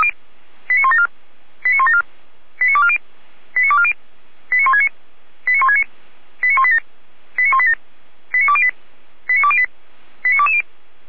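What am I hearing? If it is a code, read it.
Five-tone selective call (ZVEI), heard as quick runs of five tones, tones of about 80 ms. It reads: 89174 89174 89240 89240 89169 89169 89187 89187 89208 89208 89220
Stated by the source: ZVEI five-tone call received by SDR.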